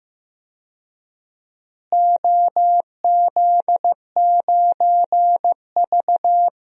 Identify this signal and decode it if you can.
Morse code, heard as a keyed tone, about 15 wpm, 700 Hz. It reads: OZ9V
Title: DXCC-100 Call Signs - 4 Characters - Repeated 2x - 15wpm